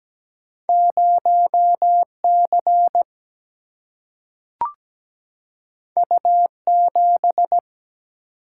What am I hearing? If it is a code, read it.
Morse code, heard as a keyed tone, about 17 wpm, 700 Hz. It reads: 0C U7
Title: Single-Letter Single-Number 17wpm